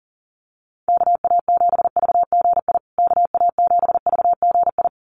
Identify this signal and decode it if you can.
Morse code, heard as a keyed tone, about 40 wpm, 700 Hz. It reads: KA7VGI KA7VGI
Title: USA Call Signs Repeated 2x 40wpm